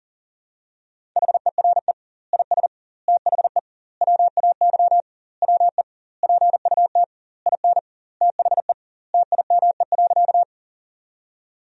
Morse code, heard as a keyed tone, about 37 wpm, 700 Hz. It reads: HERE IS THE WAY WE PUT IN THE TIME.
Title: 2x Word-Spacing - Sentences from Top 200 Words - 40wpm